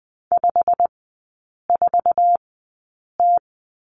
Morse code, heard as a keyed tone, about 20 wpm, spaced as for 8 wpm, 700 Hz. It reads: 54T
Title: 087 Lesson 15e - Number 5 - Three Characters 20wpm